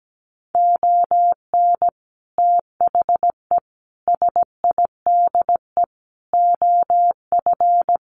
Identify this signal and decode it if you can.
Morse code, heard as a keyed tone, about 17 wpm, 700 Hz. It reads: ON THE SIDE OF
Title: Sets of 5 Words 17wpm